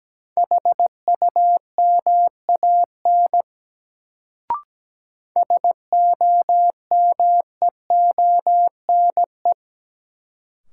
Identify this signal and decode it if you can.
Morse code, heard as a keyed tone, about 17 wpm, 700 Hz. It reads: HUMAN SOMEONE
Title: Sentences from Top 600 Words - New Word Drill - 17wpm